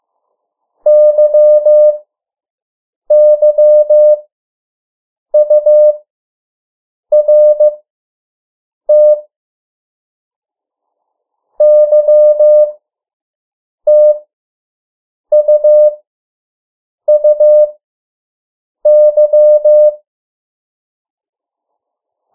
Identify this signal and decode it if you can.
Morse code, heard as a keyed tone, about 15 wpm, spaced as for 6 wpm, 600 Hz. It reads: YYURT YTUUY
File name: morse mp3 / 03 urok 2